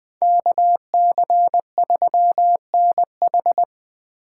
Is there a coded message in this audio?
Yes: Morse code from 0.2 to 3.6 s, heard as a keyed tone, about 20 wpm, 700 Hz. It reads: KC3NH